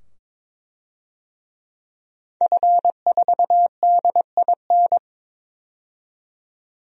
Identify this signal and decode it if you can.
Morse code, heard as a keyed tone, about 22 wpm, 700 Hz. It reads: F4DIN
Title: DXCC-100 Call Signs - 5 Characters - 22wpm